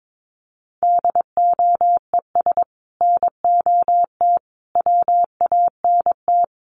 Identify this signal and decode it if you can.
Morse code, heard as a keyed tone, about 22 wpm, 700 Hz. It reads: DOES NOT WANT
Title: Sets of 5 Words Encore 22wpm